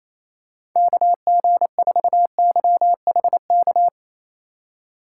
Morse code, heard as a keyed tone, about 28 wpm, 700 Hz. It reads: KG4YHK
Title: USA Call Signs - 6 Characters - 2x Character Spacing - 28wpm